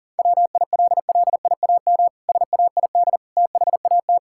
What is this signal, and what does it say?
Morse code, heard as a keyed tone, about 37 wpm, 700 Hz. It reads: WILLIAM SAID THAT
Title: Pangram Challenge v2 - 40wpm